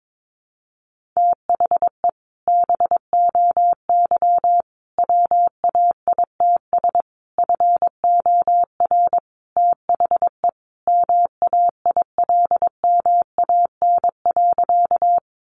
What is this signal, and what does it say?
Morse code, heard as a keyed tone, about 22 wpm, 700 Hz. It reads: THE BOY WAITS FOR THE MAILMAN.
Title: Easy Phrases 22wpm